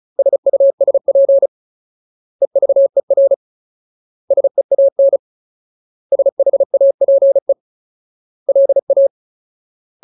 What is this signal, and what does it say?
Morse code, heard as a keyed tone, about 37 wpm, 550 Hz. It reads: SUSP EVER SEAN SHAPE LA
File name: Ultimate Word List / 35WPM Ultimate